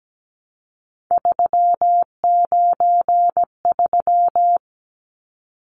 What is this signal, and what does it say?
Morse code, heard as a keyed tone, about 17 wpm, 700 Hz. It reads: 393
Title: Numbers - 3-Digit 17wpm